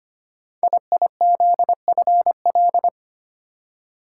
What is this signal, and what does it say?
Morse code, heard as a keyed tone, about 25 wpm, 700 Hz. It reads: IIZFL
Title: Code Groups - 5 Letters - 25wpm